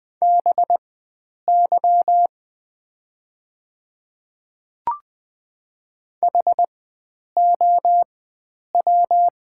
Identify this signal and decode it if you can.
Morse code, heard as a keyed tone, about 20 wpm, spaced as for 9 wpm, 690 Hz. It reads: BY HOW
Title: 4x Character Spacing - Top 100 Words - 20wpm